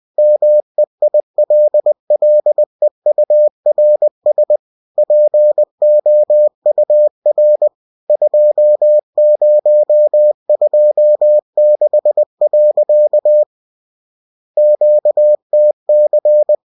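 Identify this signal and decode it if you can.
Morse code, heard as a keyed tone, about 20 wpm, 600 Hz. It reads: MEILLEURS POUR 2026. QTC
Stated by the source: Bulletin sent in CW.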